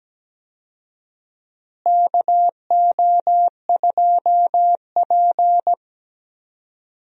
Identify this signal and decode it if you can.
Morse code, heard as a keyed tone, about 17 wpm, 700 Hz. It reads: KO2P
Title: USA Call Signs - 4 Characters - Repeated 2x - 17wpm